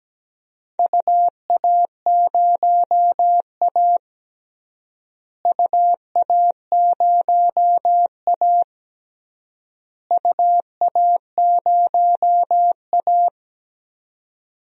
Morse code, heard as a keyed tone, about 17 wpm, 700 Hz. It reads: UA0A UA0A UA0A
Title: DXCC-100 Call Signs - 4 Characters - Repeated 3x - 17wpm